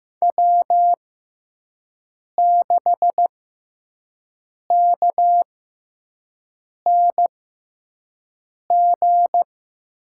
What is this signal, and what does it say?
Morse code, heard as a keyed tone, about 15 wpm, spaced as for 5 wpm, 700 Hz. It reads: W6KNG
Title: USA Call Signs - 5 Characters - 6x Character Spacing - 15wpm